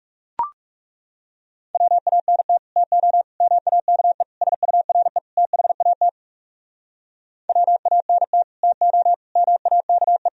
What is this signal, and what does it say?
Morse code, heard as a keyed tone, about 36 wpm, 700 Hz. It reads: WANT TO MAKE SURE THAT WANT TO MAKE
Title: Speed-Racing - Sets of 5 Words 30wpm